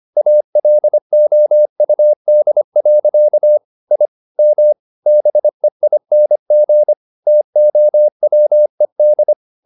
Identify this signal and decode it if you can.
Morse code, heard as a keyed tone, about 25 wpm, 600 Hz. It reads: ALOUD. I M BEING TOWED
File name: the old man and the sea part 1 25wpm